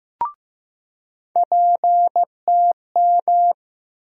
Morse code, heard as a keyed tone, about 15 wpm, 700 Hz. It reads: PTM
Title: USA Call Sign Postfixes - 3 Letters 15wpm